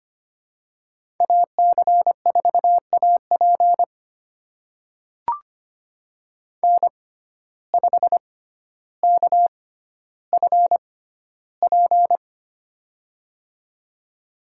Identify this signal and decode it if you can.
Morse code, heard as a keyed tone, about 25 wpm, spaced as for 13 wpm, 700 Hz. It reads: AC4AP N5KFP
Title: USA Call Signs - 5 Characters - 6x Character Spacing - 25wpm